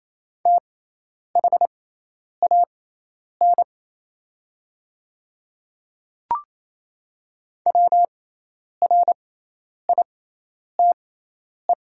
Morse code, heard as a keyed tone, about 28 wpm, spaced as for 10 wpm, 700 Hz. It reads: THAN WRITE
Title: 6x Character Spacing - Top 100 Words - 28wpm